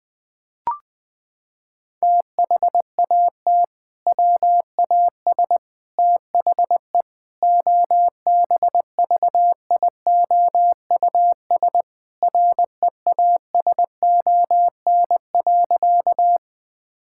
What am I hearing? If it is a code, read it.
Morse code, heard as a keyed tone, about 20 wpm, 700 Hz. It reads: THAT WAS THE OBVIOUS REASON.
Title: Sentences from Top 1600 Words 20wpm